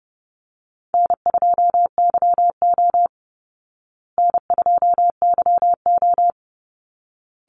Morse code, heard as a keyed tone, about 30 wpm, 700 Hz. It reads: N2YO N2YO
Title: Easy Call Signs Repeated 2x - 30wpm